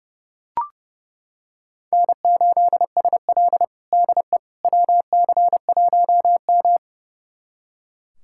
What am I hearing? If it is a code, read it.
Morse code, heard as a keyed tone, about 30 wpm, 700 Hz. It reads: N8SL DE WC1M
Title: USA Callsign DE Callsign - 4 Characters - 30wpm